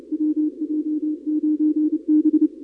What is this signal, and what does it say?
Morse code, heard as a keyed tone, about 29 wpm, 300 Hz. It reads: WJ9B
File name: contests / 2019-cw-wpx-kt5j-k5pi- -2330 0-31